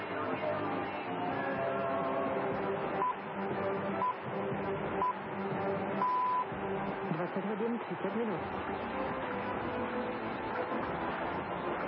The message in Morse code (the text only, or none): none